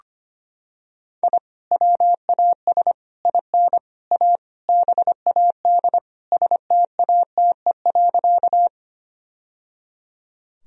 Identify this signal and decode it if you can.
Morse code, heard as a keyed tone, about 25 wpm, 700 Hz. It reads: I WAS IN A BAD STATE.
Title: Sentences from Top 1000 Words - Review Part 2 - 25wpm